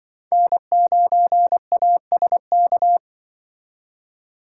Morse code, heard as a keyed tone, about 24 wpm, 700 Hz. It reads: N9ASK
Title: Speed Racing - USA Call Signs 20wpm